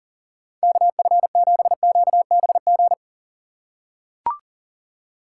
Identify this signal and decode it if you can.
Morse code, heard as a keyed tone, about 40 wpm, 700 Hz. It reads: KF7QBG
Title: USA Call Signs - 6 Characters - 2x Character Spacing - 40wpm